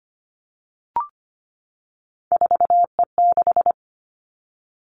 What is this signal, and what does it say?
Morse code, heard as a keyed tone, about 25 wpm, 700 Hz. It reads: 4E6
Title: Three Challenging Letter-Number Combinations 25wpm